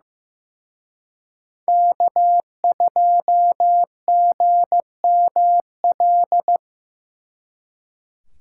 Morse code, heard as a keyed tone, about 18 wpm, 700 Hz. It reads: K2GML
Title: Speed Racing - USA 1x3 Call Signs 15wpm